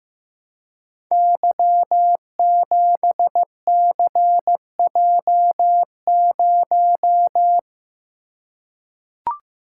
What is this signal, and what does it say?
Morse code, heard as a keyed tone, about 15 wpm, 700 Hz. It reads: Y7CJ0
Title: Code Groups - 5 Letter:Numbers - 15wpm